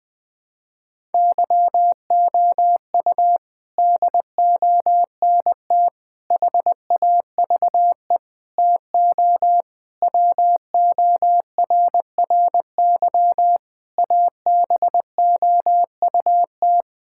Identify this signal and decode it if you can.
Morse code, heard as a keyed tone, about 20 wpm, 700 Hz. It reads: YOU DONT HAVE TO WORRY ABOUT
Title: Sets of 6 Words - 20wpm